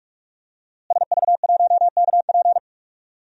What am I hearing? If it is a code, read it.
Morse code, heard as a keyed tone, about 43 wpm, 700 Hz. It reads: SV1KP
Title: DXCC-100 Call Signs - 5 Characters - Repeated 2x - 45wpm